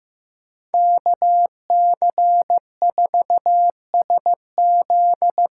Morse code, heard as a keyed tone, about 15 wpm, 700 Hz. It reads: KC4SZ
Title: USA Call Signs - 6 Characters - 6x Character Spacing - 15wpm